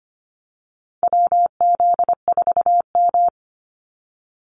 Morse code, heard as a keyed tone, about 25 wpm, 700 Hz. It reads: WZ4M
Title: NA SOTA Activator Callsigns 25wpm